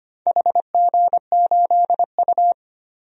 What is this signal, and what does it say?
Morse code, heard as a keyed tone, about 25 wpm, 700 Hz. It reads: HG8U
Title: DXCC-100 Call Signs - 4 Characters - Repeated 3x - 25wpm